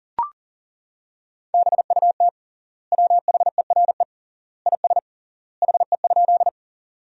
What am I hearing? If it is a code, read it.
Morse code, heard as a keyed tone, about 40 wpm, 700 Hz. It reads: BUT WHERE IS HE?
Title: 3x Word-Spacing - Sentences from Top 200 Words - 40wpm